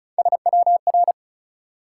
Morse code, heard as a keyed tone, about 36 wpm, 700 Hz. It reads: SWR
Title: Top 180 QSO Elements 35wpm